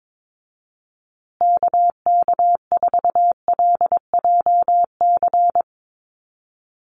Morse code, heard as a keyed tone, about 22 wpm, 700 Hz. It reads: KK4LJC
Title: USA Call Signs Repeated 3x 22wpm